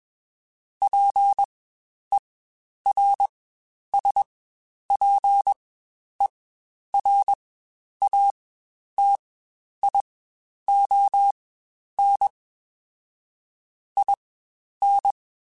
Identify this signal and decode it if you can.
Morse code, heard as a keyed tone, about 21 wpm, spaced as for 10 wpm, 800 Hz. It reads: PERSPERATION IN
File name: Quotes Spaced / Quotes Spaced-mp3s / Persperation sp @20